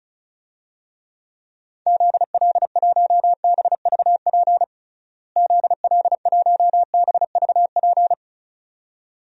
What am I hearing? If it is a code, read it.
Morse code, heard as a keyed tone, about 36 wpm, 700 Hz. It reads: ZL1BVP ZL1BVP
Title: DXCC-100 Call Signs - 6 Characters - Repeated 2x - 35wpm